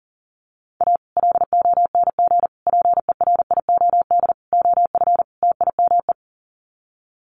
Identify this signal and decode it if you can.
Morse code, heard as a keyed tone, about 40 wpm, 700 Hz. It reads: A LONG PERIOD OF TIME
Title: Sets of 5 Words 40wpm